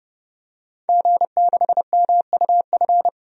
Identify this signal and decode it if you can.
Morse code, heard as a keyed tone, about 30 wpm, 700 Hz. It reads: G6MUF
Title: DXCC-100 Call Signs - 5 Characters - 30wpm